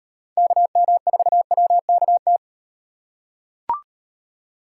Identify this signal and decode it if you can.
Morse code, heard as a keyed tone, about 38 wpm, 700 Hz. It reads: KM4WKT